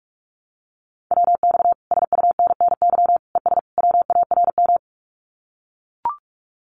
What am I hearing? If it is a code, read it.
Morse code, heard as a keyed tone, about 44 wpm, 700 Hz. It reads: WX SUNNY ES WARM
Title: Common QSO Phrases Encore 45wpm